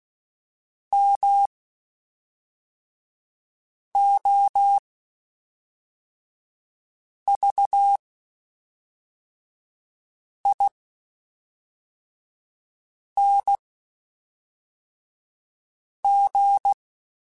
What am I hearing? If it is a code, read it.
Morse code, heard as a keyed tone, about 16 wpm, spaced as for 3 wpm, 800 Hz. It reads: MOVING
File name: (spaced) / QSOs Spaced-mp3s / WD0BGP 1 @5 sp